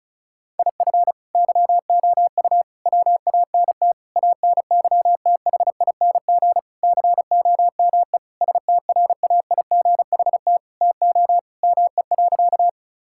Morse code, heard as a keyed tone, about 36 wpm, 700 Hz. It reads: IF YOU WANT ANYTHING COME STRAIGHT TO ME.
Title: Sentences from Top 1000 Words - Review Part 6 - 35wpm